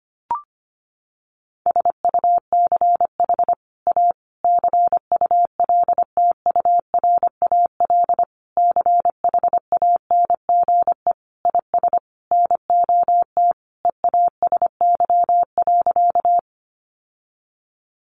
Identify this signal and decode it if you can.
Morse code, heard as a keyed tone, about 25 wpm, 700 Hz. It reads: SUCH A CULTURAL CHANGE IS NOT EASY.
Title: Sentences from Top 1500 Words 25wpm